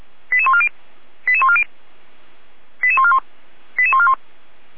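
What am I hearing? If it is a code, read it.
Five-tone selective call (ZVEI), heard as quick runs of five tones, tones of about 70 ms. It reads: 88140 88140 88141 88141
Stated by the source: ZVEI five-tone selective call received by SDR.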